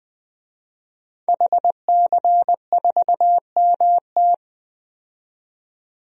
Morse code, heard as a keyed tone, about 20 wpm, 700 Hz. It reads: HC4MT
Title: DXCC-100 Call Signs - 5 Characters - 20wpm